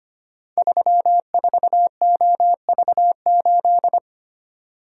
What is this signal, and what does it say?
Morse code, heard as a keyed tone, about 25 wpm, 700 Hz. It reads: 34OV8
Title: Code Groups - 5 Letter:Numbers - 25wpm